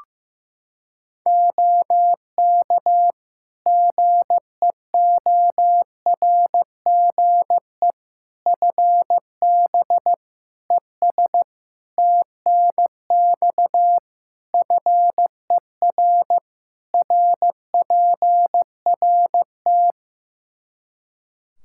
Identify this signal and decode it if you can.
Morse code, heard as a keyed tone, about 15 wpm, 700 Hz. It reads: OK GEORGE FB ES TNX FER RPRT
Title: K1USN SST QSO Practice 15wpm